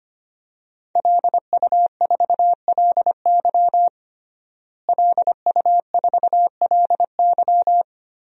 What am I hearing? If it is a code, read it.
Morse code, heard as a keyed tone, about 25 wpm, 700 Hz. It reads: LU4LY LU4LY
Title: DXCC-100 Call Signs - 5 Characters - Repeated 3x - 25wpm